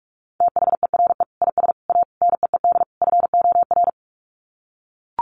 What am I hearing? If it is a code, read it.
Morse code, heard as a keyed tone, about 44 wpm, 700 Hz. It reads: THERE IS A NEED FOR
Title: Sets of 5 Words Encore2 45wpm